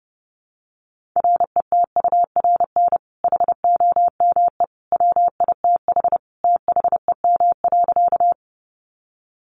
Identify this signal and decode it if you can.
Morse code, heard as a keyed tone, about 30 wpm, 700 Hz. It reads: RETURN HOME WITH THEM.
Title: Sentences from Top 800 Words 30wpm